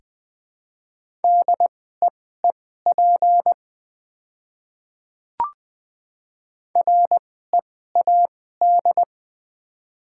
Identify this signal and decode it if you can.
Morse code, heard as a keyed tone, about 20 wpm, spaced as for 14 wpm, 700 Hz. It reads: DEEP READ
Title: 2x Character Spacing - Top 500 Words - 20wpm